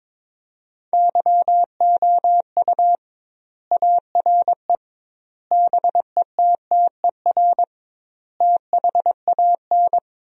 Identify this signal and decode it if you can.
Morse code, heard as a keyed tone, about 22 wpm, 700 Hz. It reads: YOU ARE BETTER THAN
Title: 2x Word-Spacing - Sentences from Top 400 Words - 22wpm